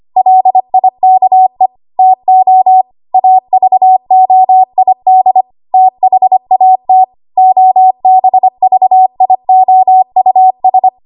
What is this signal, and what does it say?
Morse code, heard as a keyed tone, about 25 wpm, 750 Hz. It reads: LIKE TO AVOID THAT OBVIOUS
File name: Archive / 25 WPM / 25WPM